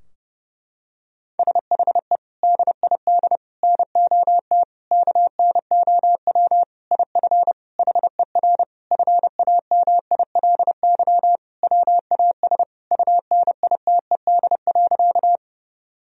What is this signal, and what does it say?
Morse code, heard as a keyed tone, about 32 wpm, 700 Hz. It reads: SHE DID NOT KNOW IF HER FAMILY WAS UNITED.
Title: Sentences from Top 1000 Words - Review Part 4 - 30wpm